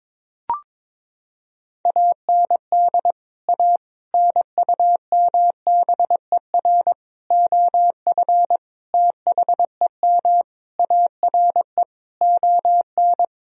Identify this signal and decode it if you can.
Morse code, heard as a keyed tone, about 21 wpm, 700 Hz. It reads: AND A NUMBER OF THEM ARE ON
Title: Sentences from Top 1000 Words - Review Part 1 - 22wpm